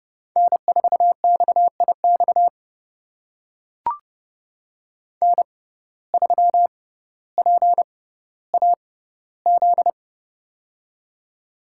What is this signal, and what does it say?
Morse code, heard as a keyed tone, about 30 wpm, spaced as for 15 wpm, 700 Hz. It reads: N4XIX N3PAZ
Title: USA Call Signs - 5 Characters - 6x Character Spacing - 30wpm